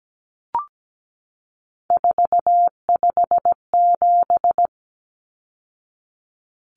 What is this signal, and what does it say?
Morse code, heard as a keyed tone, about 17 wpm, 700 Hz. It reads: 457
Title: Numbers - 3-Digit 17wpm